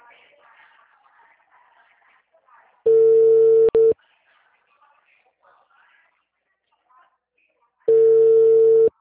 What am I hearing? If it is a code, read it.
Ringback tone (China, 450 Hz), with speech.